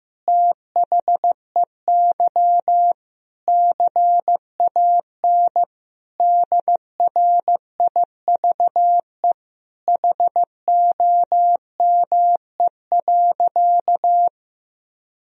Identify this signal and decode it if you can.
Morse code, heard as a keyed tone, about 15 wpm, 700 Hz. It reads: THEY CAN DRIVE HOME.
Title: Sentences from Top 1000 Words - Review Part 2 - 15wpm